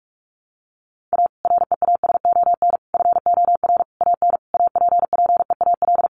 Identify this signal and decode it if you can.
Morse code, heard as a keyed tone, about 44 wpm, 700 Hz. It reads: A REASON FOR AN APPEAL
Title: Sentences from Top 1300 Words 45wpm